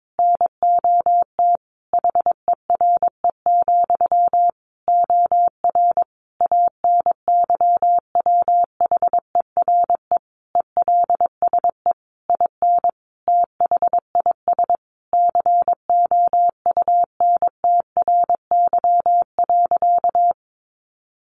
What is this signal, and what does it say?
Morse code, heard as a keyed tone, about 22 wpm, 700 Hz. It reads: NOT HERE, OR ANYWHERE ELSE IN THIS COUNTRY.
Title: Sentences from Top 1400 Words 22wpm